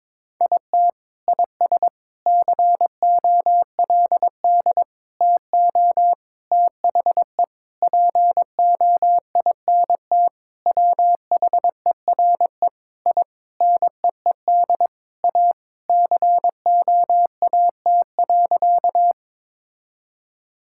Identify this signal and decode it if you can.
Morse code, heard as a keyed tone, about 22 wpm, 700 Hz. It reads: IT IS COLD TO THE POINT WHERE I NEED A COAT.
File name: practice / 22wpm / Sentences with 4-Word Phrases - 22wpm